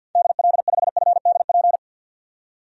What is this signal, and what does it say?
Morse code, heard as a keyed tone, about 53 wpm, 700 Hz. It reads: DL5FDP
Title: DXCC-100 Call Signs - 6 Characters - 50wpm